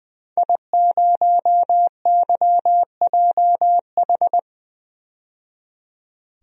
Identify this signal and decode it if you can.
Morse code, heard as a keyed tone, about 20 wpm, 700 Hz. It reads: I0YJH
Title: DXCC-100 Call Signs - 5 Characters - 20wpm